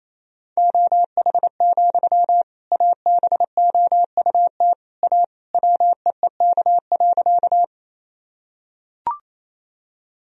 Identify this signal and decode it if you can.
Morse code, heard as a keyed tone, about 28 wpm, 700 Hz. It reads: OH, ABOUT A WEEK.
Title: Sentences from Top 1000 Words - Review Part 2 - 28wpm